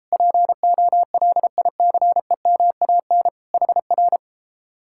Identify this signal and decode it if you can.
Morse code, heard as a keyed tone, about 37 wpm, 700 Hz. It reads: POLICEMAN HR